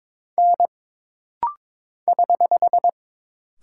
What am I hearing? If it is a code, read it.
Morse code, heard as a keyed tone, about 22 wpm, 700 Hz. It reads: N <HH>
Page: This rapid-fire timing is geared to push your ICR (Instant Character Recognition) skills.